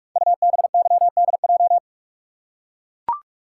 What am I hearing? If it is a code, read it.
Morse code, heard as a keyed tone, about 44 wpm, 700 Hz. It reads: UBYDJ
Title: Code Groups - 5 Letters - 45wpm